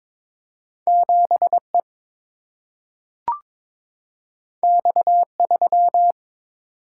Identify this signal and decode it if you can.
Morse code, heard as a keyed tone, about 22 wpm, 700 Hz. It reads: 7 E X 3
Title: Single-Letter Single-Number 22wpm